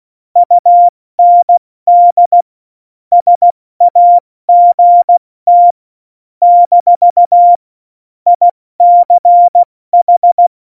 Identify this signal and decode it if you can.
Morse code, heard as a keyed tone, about 16 wpm, 700 Hz. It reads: UND SAGT - ICH
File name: Morsekurs / Witze / Witz 13 March 2018, 70 F80